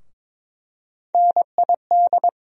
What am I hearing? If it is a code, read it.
Morse code, heard as a keyed tone, about 22 wpm, 700 Hz. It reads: NID
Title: USA Call Sign Postfixes - 3 Letters 22wpm